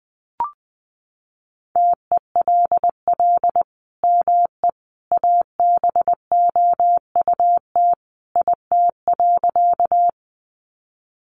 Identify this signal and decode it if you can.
Morse code, heard as a keyed tone, about 20 wpm, 700 Hz. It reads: TELL ME ABOUT IT.
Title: Sentences from Top 200 Words 20wpm